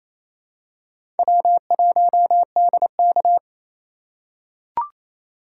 Morse code, heard as a keyed tone, about 27 wpm, 700 Hz. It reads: W1DK
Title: USA Call Signs - 4 Characters - 4x Character Spacing - 28wpm